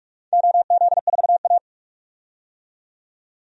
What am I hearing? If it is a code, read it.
Morse code, heard as a keyed tone, about 41 wpm, 700 Hz. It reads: OZ4A
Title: DXCC-100 Call Signs - 4 Characters - 45wpm